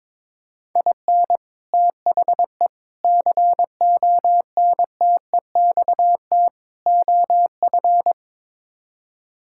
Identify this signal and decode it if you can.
Morse code, heard as a keyed tone, about 22 wpm, 700 Hz. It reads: IN THE CONTEXT OF
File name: practice / 22wpm / Build A Phrase - 5-Word Phrases - 22wpm